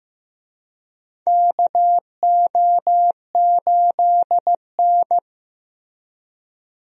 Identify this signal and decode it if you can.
Morse code, heard as a keyed tone, about 15 wpm, 700 Hz. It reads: KO8N
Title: USA Call Signs - 4 Characters - Repeated 3x - 15wpm